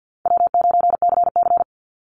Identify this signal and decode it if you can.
Morse code, heard as a keyed tone, about 49 wpm, 700 Hz. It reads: W9CC
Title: Easy Call Signs Repeated 2x - 50wpm